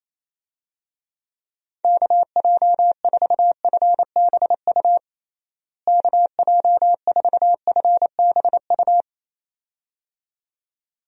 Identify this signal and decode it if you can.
Morse code, heard as a keyed tone, about 28 wpm, 700 Hz. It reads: KJ4FBU KJ4FBU
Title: USA Call Signs - 6 Characters - Repeated 2x - 28wpm